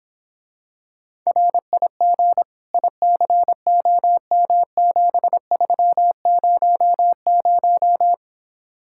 Morse code, heard as a keyed tone, about 26 wpm, 700 Hz. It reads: RIG ICOM7300